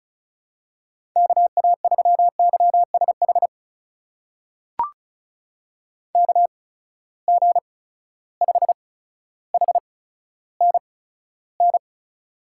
Morse code, heard as a keyed tone, about 36 wpm, spaced as for 15 wpm, 700 Hz. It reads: KA3YSH KG5HNN
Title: USA Call Signs - 6 Characters - 8x Character Spacing - 35wpm